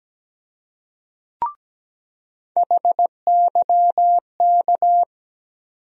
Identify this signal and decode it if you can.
Morse code, heard as a keyed tone, about 17 wpm, 700 Hz. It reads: HYK
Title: USA Call Sign Postfixes - 3 Letters 17wpm